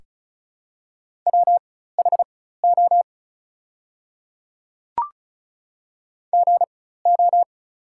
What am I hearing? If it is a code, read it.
Morse code, heard as a keyed tone, about 34 wpm, spaced as for 16 wpm, 700 Hz. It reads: WHO GO
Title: 4x Character Spacing - Top 100 Words - 35wpm